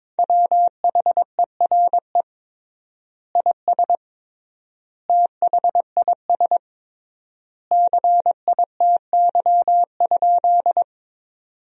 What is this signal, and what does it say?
Morse code, heard as a keyed tone, about 22 wpm, 700 Hz. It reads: WHERE IS THIS CITY?
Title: 3x Word-Spacing - Sentences from Top 300 Words - 22wpm